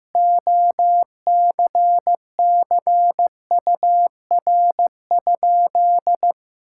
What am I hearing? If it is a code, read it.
Morse code, heard as a keyed tone, about 15 wpm, 700 Hz. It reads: OCCUR?
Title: Sentences from Top 1000 Words - Review Part 6 - 15wpm